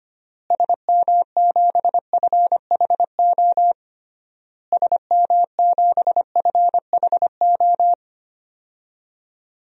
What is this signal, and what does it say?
Morse code, heard as a keyed tone, about 25 wpm, 700 Hz. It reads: SM7FHO SM7FHO
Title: DXCC-100 Call Signs - 6 Characters - Repeated 3x - 25wpm